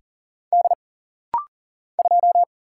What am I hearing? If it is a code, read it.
Morse code, heard as a keyed tone, about 39 wpm, 700 Hz. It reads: D 2
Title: Single Letter-Number - Rapid-Fire 40wpm